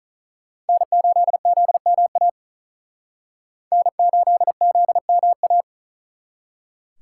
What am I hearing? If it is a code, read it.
Morse code, heard as a keyed tone, about 46 wpm, 700 Hz. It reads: N8ZMA N8ZMA